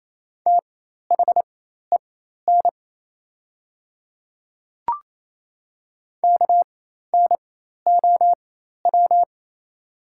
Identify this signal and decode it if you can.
Morse code, heard as a keyed tone, about 28 wpm, spaced as for 13 wpm, 700 Hz. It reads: THEN KNOW
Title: 4x Character Spacing - Top 100 Words - 28wpm